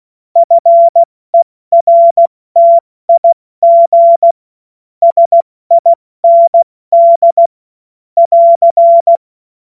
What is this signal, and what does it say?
Morse code, heard as a keyed tone, about 16 wpm, 680 Hz. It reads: FERTIG SIND <AR>
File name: Morsekurs / Witze / Witz 23. Dezember 2025 70 F80